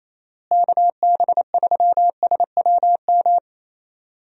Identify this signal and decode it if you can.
Morse code, heard as a keyed tone, about 28 wpm, 700 Hz. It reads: KB3SWM
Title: USA Call Signs - 6 Characters - Repeated 3x - 28wpm